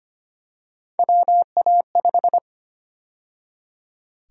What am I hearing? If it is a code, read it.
Morse code, heard as a keyed tone, about 25 wpm, 700 Hz. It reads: WA5
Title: USA Call Sign Prefixes - 25wpm